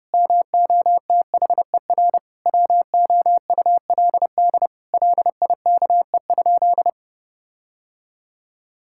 Morse code, heard as a keyed tone, about 30 wpm, 700 Hz. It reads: MOTHER WOULD LIKE?
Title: Sentences from Top 1600 Words 30wpm